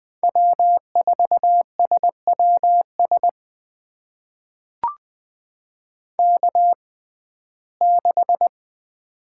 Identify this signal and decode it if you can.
Morse code, heard as a keyed tone, about 21 wpm, spaced as for 15 wpm, 700 Hz. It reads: W4SWS K6
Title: USA Call Signs - 5 Characters - 6x Character Spacing - 20wpm